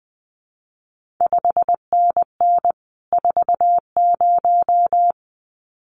Morse code, heard as a keyed tone, about 20 wpm, 700 Hz. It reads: 5NN 40
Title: CQ WW DX Contest Exchange Encore - 20wpm